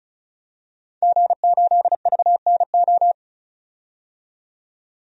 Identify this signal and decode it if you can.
Morse code, heard as a keyed tone, about 36 wpm, 700 Hz. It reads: G8VNO
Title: DXCC-100 Call Signs - 5 Characters - 35wpm